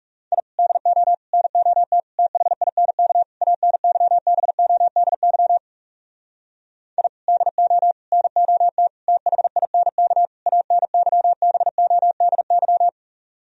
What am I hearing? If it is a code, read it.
Morse code, heard as a keyed tone, about 40 wpm, 700 Hz. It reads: I DO NOT THINK ANYBODY I DO NOT THINK ANYBODY